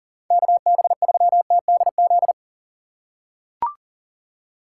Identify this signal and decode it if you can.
Morse code, heard as a keyed tone, about 40 wpm, 700 Hz. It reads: KB3TDZ
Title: USA Call Signs - 6 Characters - 2x Character Spacing - 40wpm